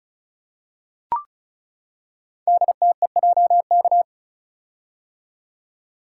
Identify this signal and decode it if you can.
Morse code, heard as a keyed tone, about 36 wpm, 700 Hz. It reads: DTEJK
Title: Code Groups - 5 Letters - 35wpm